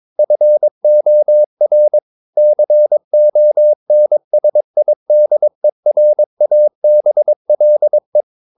Morse code, heard as a keyed tone, about 22 wpm, 600 Hz. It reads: FOR CONSIDERABLE
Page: Romanen Never af Ken Follet på engelsk.